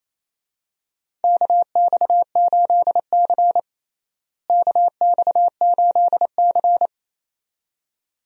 Morse code, heard as a keyed tone, about 28 wpm, 700 Hz. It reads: KX8C KX8C
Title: USA Call Signs - 4 Characters - Repeated 2x - 28wpm